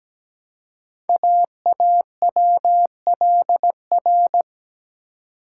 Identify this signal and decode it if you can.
Morse code, heard as a keyed tone, about 17 wpm, 700 Hz. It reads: AAWLR
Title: Code Groups - 5 Letters - 17wpm